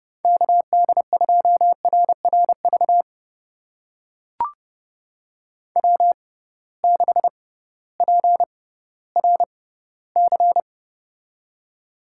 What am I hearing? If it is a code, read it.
Morse code, heard as a keyed tone, about 31 wpm, spaced as for 16 wpm, 700 Hz. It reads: KD2RRV W6PRC
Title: USA Call Signs - 6x Character Spacing - 30wpm